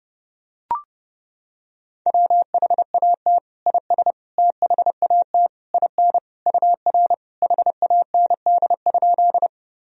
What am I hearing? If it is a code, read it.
Morse code, heard as a keyed tone, about 30 wpm, 700 Hz. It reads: WHAT IS THAT IN UR HAND?
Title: Sentences from Top 200 Words and Abbreviations 30wpm